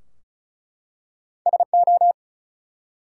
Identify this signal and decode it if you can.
Morse code, heard as a keyed tone, about 33 wpm, 700 Hz. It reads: SO
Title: Two Letter Words - 35wpm